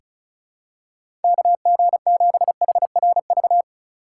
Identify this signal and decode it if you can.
Morse code, heard as a keyed tone, about 34 wpm, 700 Hz. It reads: KG7HRV
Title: USA Call Signs - 2x Character Spacing - 35wpm